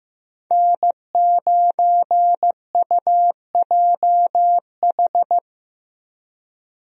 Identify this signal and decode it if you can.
Morse code, heard as a keyed tone, about 15 wpm, 700 Hz. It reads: N9UJH
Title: USA Call Signs - 5 Characters - Repeated 3x - 15wpm